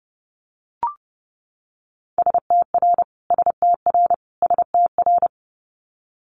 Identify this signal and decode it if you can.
Morse code, heard as a keyed tone, about 31 wpm, 700 Hz. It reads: STR STR STR
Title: Multi-letter Phonemes Repeated 3x 30wpm